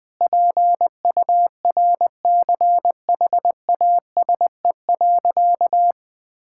Morse code, heard as a keyed tone, about 20 wpm, 700 Hz. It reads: PURCHASE.
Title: Sentences from Top 1000 Words - Review Part 5 - 20wpm